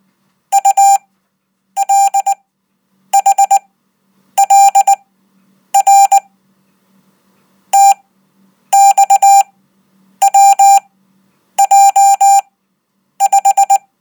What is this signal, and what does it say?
Morse code, heard as a keyed tone, about 19 wpm, spaced as for 8 wpm, 790 Hz. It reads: ULHLR TXWJ5
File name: Code